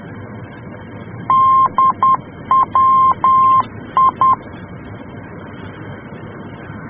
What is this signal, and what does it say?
Morse code, heard as a keyed tone, about 10 wpm, 1060 Hz. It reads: DWI
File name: germany / mx / DWI-D